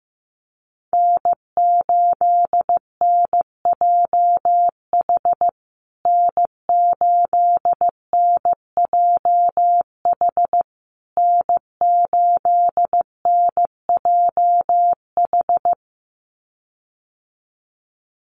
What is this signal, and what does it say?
Morse code, heard as a keyed tone, about 15 wpm, 700 Hz. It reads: N8NJH N8NJH N8NJH
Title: USA Call Signs Repeated 3x 15wpm